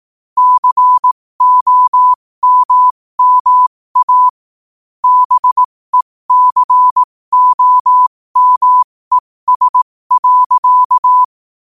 Morse code, heard as a keyed tone, about 18 wpm, spaced as for 15 wpm, 1000 Hz. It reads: COMMA BECOMES.
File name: qotdfiles / 15wpm